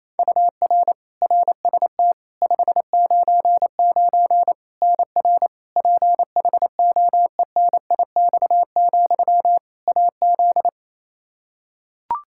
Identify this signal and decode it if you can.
Morse code, heard as a keyed tone, about 28 wpm, 700 Hz. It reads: UR RST 599 NR PHOENIX, AZ
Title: Practice QSO - Exchange 1 - Beginning - 28wpm